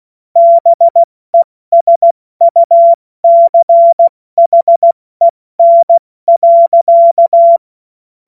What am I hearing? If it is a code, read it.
Morse code, heard as a keyed tone, about 16 wpm, 680 Hz. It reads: BESUCHEN.
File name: Morsekurs / Witze / Witz 20. Dezember 2022 70 F80